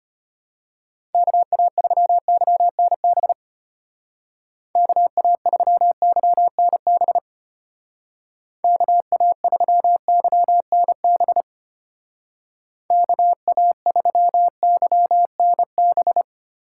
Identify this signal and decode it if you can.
Morse code, heard as a keyed tone, about 32 wpm, 700 Hz. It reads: KA3YNB KA3YNB KA3YNB KA3YNB